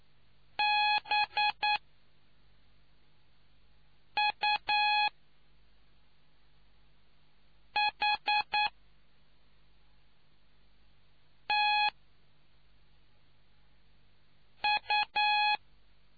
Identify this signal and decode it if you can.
Morse code, heard as a keyed tone, about 9 wpm, spaced as for 3 wpm, 840 Hz. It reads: BUHTU